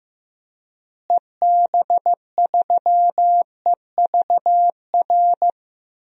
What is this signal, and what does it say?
Morse code, heard as a keyed tone, about 15 wpm, 700 Hz. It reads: EB3EVR
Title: DXCC-100 Call Signs - 6 Characters - Repeated 3x - 15wpm